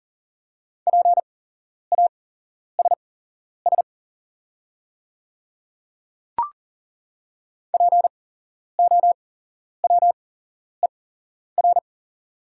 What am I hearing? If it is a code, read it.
Morse code, heard as a keyed tone, about 39 wpm, spaced as for 11 wpm, 690 Hz. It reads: PASS POWER
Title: 8x Character Spacing - Top 500 Words - 40wpm